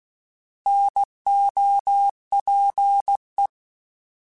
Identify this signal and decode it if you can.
Morse code, heard as a keyed tone, about 16 wpm, 800 Hz. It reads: NOPE